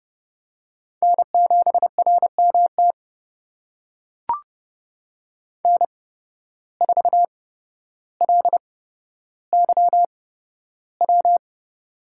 Morse code, heard as a keyed tone, about 32 wpm, spaced as for 13 wpm, 700 Hz. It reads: N7RMT N4LYW
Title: USA Call Signs - 5 Characters - 8x Character Spacing - 30wpm